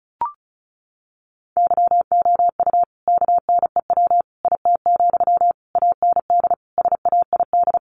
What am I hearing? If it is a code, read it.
Morse code, heard as a keyed tone, about 34 wpm, 700 Hz. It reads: YOU KNEW IT, AND SAID
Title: Sentences from Top 500 Words 35wpm